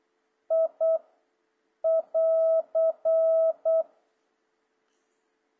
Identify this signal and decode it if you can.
Morse code, heard as a keyed tone, about 8 wpm, 650 Hz. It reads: I <AR>
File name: jh8cbh / 2026cw